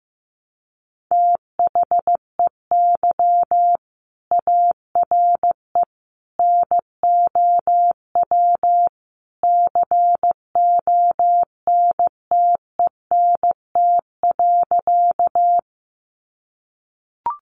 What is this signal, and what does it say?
Morse code, heard as a keyed tone, about 15 wpm, 700 Hz. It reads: THEY ARE NOW CONTENT.
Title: Sentences from Top 900 Words 15wpm